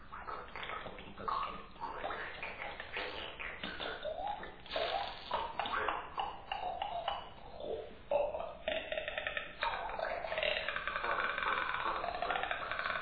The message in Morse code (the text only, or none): none